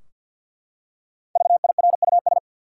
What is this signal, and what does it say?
Morse code, heard as a keyed tone, about 52 wpm, 700 Hz. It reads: VIRUS
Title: Five Letter Words - 50wpm